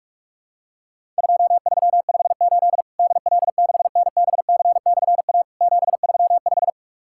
Morse code, heard as a keyed tone, about 44 wpm, 700 Hz. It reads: 2358 DL6NBC/A 735
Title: Worked All Europe DX Contest - QTC Exchange - 45wpm